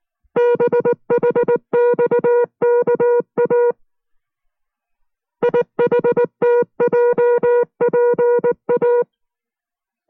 Morse code, heard as a keyed tone, about 19 wpm, 480 Hz. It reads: BHXKA IHTJPA